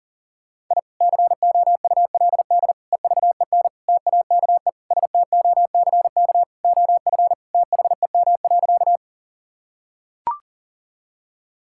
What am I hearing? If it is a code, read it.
Morse code, heard as a keyed tone, about 40 wpm, 690 Hz. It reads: I COULD EVEN TAKE STOCK OF THEM.
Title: Sentences from Top 1000 Words - Review Part 5 - 40wpm